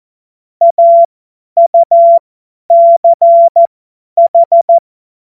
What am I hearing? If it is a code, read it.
Morse code, heard as a keyed tone, about 14 wpm, spaced as for 10 wpm, 680 Hz. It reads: AUCH
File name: Morsekurs / Witze / Witz 29 April 2025 50 F70